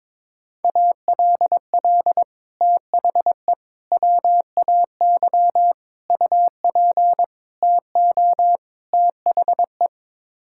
Morse code, heard as a keyed tone, about 22 wpm, 700 Hz. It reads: ALL THE WAY UP TO THE
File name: practice / 22wpm / Sets of 6 Words - 22wpm